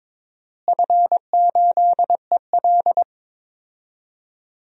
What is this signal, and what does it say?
Morse code, heard as a keyed tone, about 22 wpm, 700 Hz. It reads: F8EL
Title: DXCC-100 Call Signs - 4 Characters - Repeated 3x - 22wpm